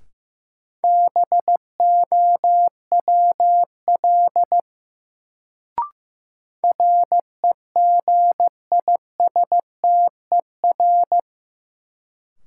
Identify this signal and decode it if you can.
Morse code, heard as a keyed tone, about 15 wpm, 700 Hz. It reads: BOWL REGISTER
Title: Sentences from Top 1200 Words - New Word Drill Alt - 15wpm